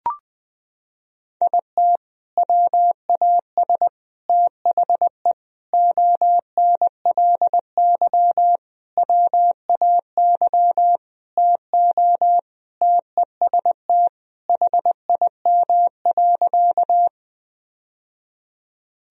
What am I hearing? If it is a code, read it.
Morse code, heard as a keyed tone, about 20 wpm, 700 Hz. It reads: IT WAS THE ONLY WAY TO TEST HIM.
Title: Sentences from Top 1000 Words - Review Part 3 - 20wpm